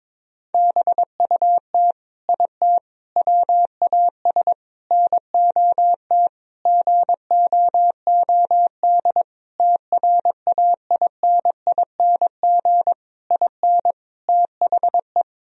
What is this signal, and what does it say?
Morse code, heard as a keyed tone, about 22 wpm, 700 Hz. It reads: BUT IT WAS NOT GOOD TRAINING IN THE
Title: Sentences from Top 1000 Words - Review Part 4 - 22wpm